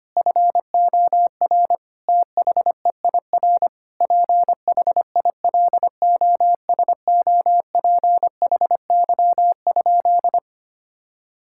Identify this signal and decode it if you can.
Morse code, heard as a keyed tone, about 25 wpm, 700 Hz. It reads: FOR THEIR PHILOSOPHY?
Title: Sentences from Top 1000 Words - Review Part 6 - 25wpm